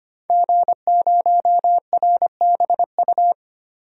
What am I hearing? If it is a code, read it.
Morse code, heard as a keyed tone, about 25 wpm, 700 Hz. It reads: G0RBU
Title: DXCC-100 Call Signs - 5 Characters - Repeated 3x - 25wpm